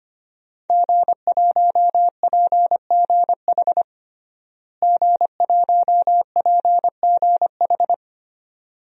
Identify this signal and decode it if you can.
Morse code, heard as a keyed tone, about 25 wpm, 700 Hz. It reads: G1PGH G1PGH
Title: DXCC-100 Call Signs - 5 Characters - Repeated 2x - 25wpm